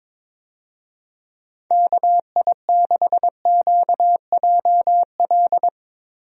Morse code, heard as a keyed tone, about 21 wpm, 700 Hz. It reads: KI6QJL
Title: USA Call Signs - 6 Characters - Repeated 2x - 22wpm